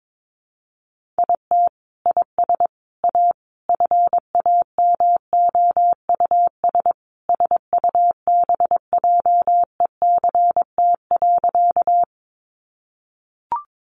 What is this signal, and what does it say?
Morse code, heard as a keyed tone, about 22 wpm, 700 Hz. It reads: IT IS A FAMOUS SUBJECT.
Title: Sentences from Top 1500 Words 22wpm